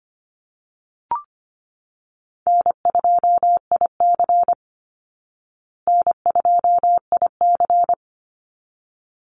Morse code, heard as a keyed tone, about 25 wpm, 700 Hz. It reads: N2IC N2IC
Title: Call Signs Repeated 3x - 25wpm